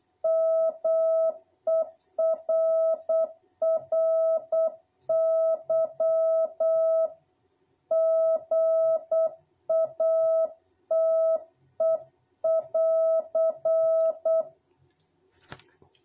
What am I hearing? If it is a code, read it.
Morse code, heard as a keyed tone, about 8 wpm, 660 Hz. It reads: MERRY GATE<AR>
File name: jh8cbh / 2021cw